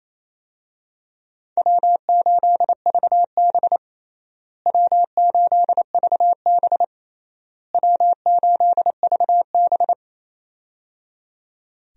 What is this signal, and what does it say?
Morse code, heard as a keyed tone, about 28 wpm, 700 Hz. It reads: W8VB W8VB W8VB
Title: USA Call Signs - 4 Characters - Repeated 3x - 28wpm